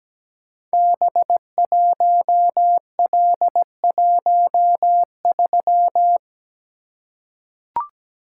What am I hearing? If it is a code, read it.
Morse code, heard as a keyed tone, about 17 wpm, 700 Hz. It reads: B1L13
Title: Code Groups - 5 Letter:Numbers - 17wpm